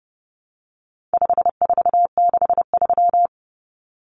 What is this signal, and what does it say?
Morse code, heard as a keyed tone, about 30 wpm, 700 Hz. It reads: 5463
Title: Numbers - 4-Digit 30wpm